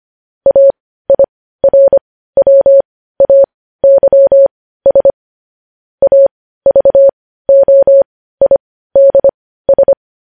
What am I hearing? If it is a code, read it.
Morse code, heard as a keyed tone, about 25 wpm, spaced as for 15 wpm, 550 Hz. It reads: AIRWAYS AVOIDS
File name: podcast / News headlines-1x-25x15